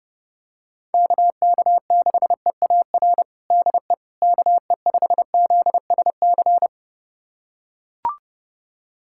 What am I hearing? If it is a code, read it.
Morse code, heard as a keyed tone, about 30 wpm, 700 Hz. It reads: KK6EAR DE KE5ZSC
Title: USA Callsign DE Callsign 30wpm